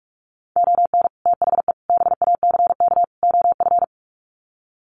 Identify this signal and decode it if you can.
Morse code, heard as a keyed tone, about 44 wpm, 700 Hz. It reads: ON THE BACK OF
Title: Sets of 4 Words 45wpm